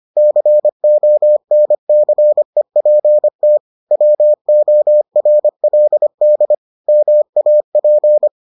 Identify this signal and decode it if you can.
Morse code, heard as a keyed tone, about 25 wpm, 600 Hz. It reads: CONCEPT WORLD MAP